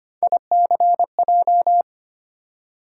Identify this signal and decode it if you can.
Morse code, heard as a keyed tone, about 25 wpm, 700 Hz. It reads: ICJ
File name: Three Letter-Number Combinations - 25wpm